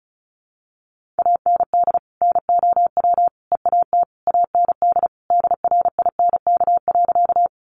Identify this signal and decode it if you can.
Morse code, heard as a keyed tone, about 34 wpm, 700 Hz. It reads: AND NOW EAT AND DRINK.
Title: Sentences from Top 1500 Words 35wpm